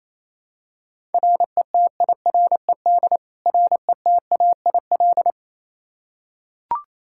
Morse code, heard as a keyed tone, about 28 wpm, 700 Hz. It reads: RETIRED RETAIL
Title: K1USN SST QSO Practice 28wpm